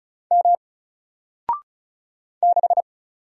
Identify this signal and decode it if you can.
Morse code, heard as a keyed tone, about 34 wpm, 700 Hz. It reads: M 6
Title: Single Letter-Number Alt 35wpm